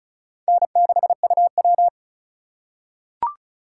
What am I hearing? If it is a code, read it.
Morse code, heard as a keyed tone, about 36 wpm, 700 Hz. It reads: N6UW
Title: USA Call Signs - 4 Characters - 3x Character Spacing - 35wpm